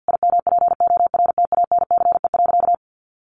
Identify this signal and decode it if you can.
Morse code, heard as a keyed tone, about 50 wpm, 700 Hz. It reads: IMPORTANCE.
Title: Sentences from Top 1200 Words 50wpm